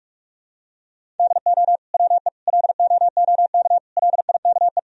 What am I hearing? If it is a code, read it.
Morse code, heard as a keyed tone, about 41 wpm, 700 Hz. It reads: DO WE LOOK LIKE
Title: Sentences from Top 1000 Words - Review Part 4 - 45wpm